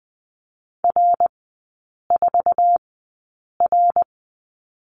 Morse code, heard as a keyed tone, about 20 wpm, spaced as for 8 wpm, 700 Hz. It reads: R4R
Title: 081 Lesson 14e - Number 2 - Three Characters 20wpm